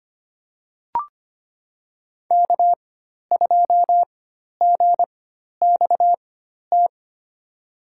Morse code, heard as a keyed tone, about 25 wpm, spaced as for 12 wpm, 700 Hz. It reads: K2GXT
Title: USA Call Signs - 4x Character Spacing - 25wpm